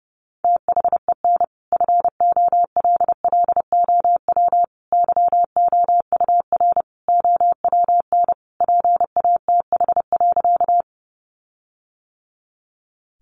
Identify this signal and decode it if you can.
Morse code, heard as a keyed tone, about 30 wpm, 700 Hz. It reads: THEN FOLLOW YOUR OWN PATH.
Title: Sentences from Top 1400 Words 30wpm